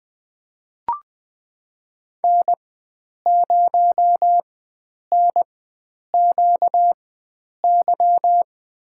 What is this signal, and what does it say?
Morse code, heard as a keyed tone, about 20 wpm, spaced as for 9 wpm, 700 Hz. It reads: N0NQY
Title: USA Call Signs - 5 Characters - 4x Character Spacing - 20wpm